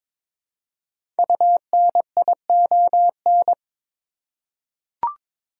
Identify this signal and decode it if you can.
Morse code, heard as a keyed tone, about 22 wpm, 700 Hz. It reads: UNION
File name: 5 Letter Words - First 3 Letters Given - 22wpm